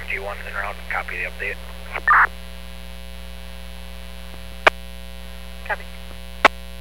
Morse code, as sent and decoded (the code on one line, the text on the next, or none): none